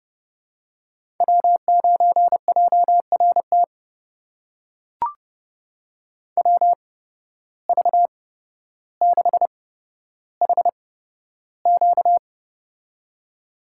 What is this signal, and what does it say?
Morse code, heard as a keyed tone, about 30 wpm, spaced as for 13 wpm, 700 Hz. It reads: W9JRT WV6HQ
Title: USA Call Signs - 5 Characters - 8x Character Spacing - 30wpm